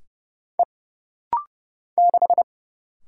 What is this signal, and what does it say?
Morse code, heard as a keyed tone, about 30 wpm, 700 Hz. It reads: E 6
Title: Single Letter-Number Alt - Rapid-Fire 30wpm